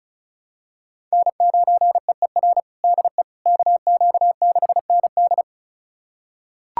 Morse code, heard as a keyed tone, about 34 wpm, 700 Hz. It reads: N9EER DE KQ6ND
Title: USA Callsign DE Callsign - 5 Characters - 35wpm